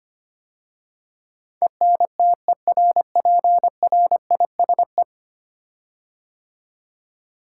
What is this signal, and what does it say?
Morse code, heard as a keyed tone, about 25 wpm, 700 Hz. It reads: ENTERPRISE
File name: 10 Letter Words - First 3 Letters Given - 25wpm